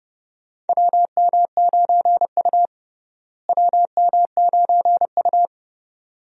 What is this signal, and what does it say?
Morse code, heard as a keyed tone, about 30 wpm, 700 Hz. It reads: WM9U WM9U
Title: USA Call Signs - 4 Characters - Repeated 2x - 30wpm